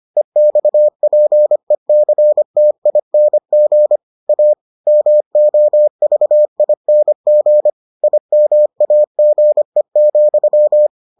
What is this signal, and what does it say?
Morse code, heard as a keyed tone, about 25 wpm, 600 Hz. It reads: EXPECTING A MOVING IMAGE,